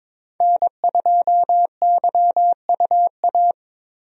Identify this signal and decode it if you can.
Morse code, heard as a keyed tone, about 22 wpm, 700 Hz. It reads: N2YUA
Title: USA Call Signs - 5 Characters - 22wpm